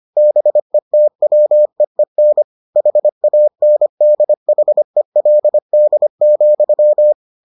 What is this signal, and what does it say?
Morse code, heard as a keyed tone, about 25 wpm, 600 Hz. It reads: BETWEEN HANDHELD,